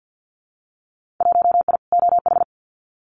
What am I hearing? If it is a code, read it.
Morse code, heard as a keyed tone, about 48 wpm, 700 Hz. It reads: 1I OH
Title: Winter Field Day Exchanges 50wpm